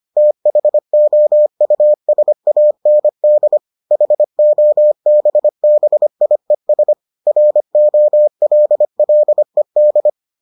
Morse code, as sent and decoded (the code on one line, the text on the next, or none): - .... --- ..- ... .- -. -.. / .... --- -... -... .. ... / .-. --- .-.. .-.. -..
THOUSAND HOBBIES ROLLED